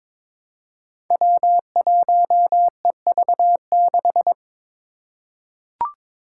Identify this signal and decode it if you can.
Morse code, heard as a keyed tone, about 23 wpm, 700 Hz. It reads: W1EV6
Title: Five Challenging Letter-Number Combinations 22wpm